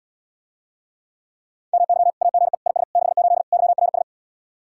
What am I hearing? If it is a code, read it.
Morse code, heard as a keyed tone, about 80 wpm, 700 Hz. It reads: MY AGE IS 68 YRS